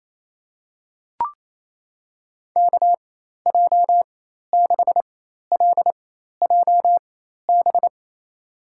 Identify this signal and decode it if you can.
Morse code, heard as a keyed tone, about 28 wpm, spaced as for 13 wpm, 700 Hz. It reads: KJ6LJB
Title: USA Call Signs - 6 Characters - 4x Character Spacing - 28wpm